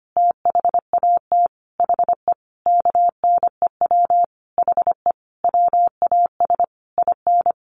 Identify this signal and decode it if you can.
Morse code, heard as a keyed tone, about 25 wpm, 700 Hz. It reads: THAT HE KNEW HE WAS IN
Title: Sentences from Top 1300 Words 25wpm